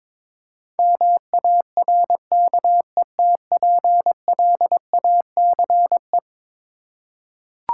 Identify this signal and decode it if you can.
Morse code, heard as a keyed tone, about 22 wpm, 700 Hz. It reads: MARKETPLACE
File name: practice / 22wpm / Eleven Letter Words - 22wpm